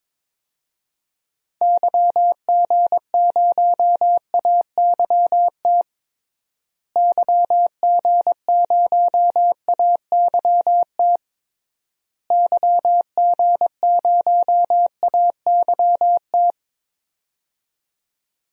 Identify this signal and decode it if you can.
Morse code, heard as a keyed tone, about 22 wpm, 700 Hz. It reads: YG0AYT YG0AYT YG0AYT
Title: DXCC-100 Call Signs - 6 Characters - Repeated 3x - 22wpm